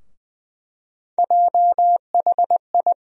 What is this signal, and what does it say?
Morse code, heard as a keyed tone, about 20 wpm, 700 Hz. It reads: JHI
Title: USA Call Sign Postfixes - 3 Letters 20wpm